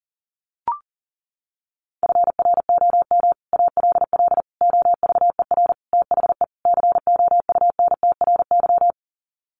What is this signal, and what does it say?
Morse code, heard as a keyed tone, about 40 wpm, 700 Hz. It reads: FROM ALL OVER THE COUNTRY
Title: Sets of 5 Words Encore 40wpm